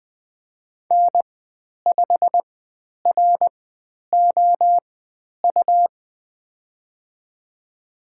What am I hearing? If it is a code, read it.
Morse code, heard as a keyed tone, about 20 wpm, spaced as for 10 wpm, 700 Hz. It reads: N5ROU
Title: 102 Lesson 17g - Letter u - Call Signs 20wpm